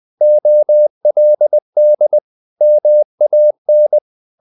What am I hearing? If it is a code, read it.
Morse code, heard as a keyed tone, about 20 wpm, 600 Hz. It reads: OLD MAN
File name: the old man and the sea / the old man and the sea part 1 20wpm